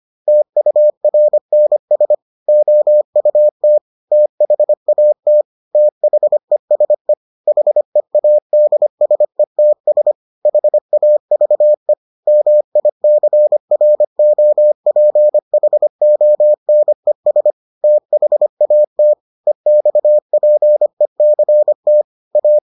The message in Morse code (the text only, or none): TURNS OUT THAT THESE HEADSETS HAVE MICROPHONES THAT EXPECT A